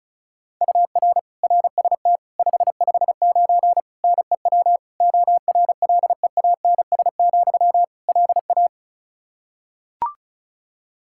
Morse code, heard as a keyed tone, about 34 wpm, 700 Hz. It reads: UR RST 559 NEW ORLEANS, LA
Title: Practice QSO - Exchange 1 - Beginning - 35wpm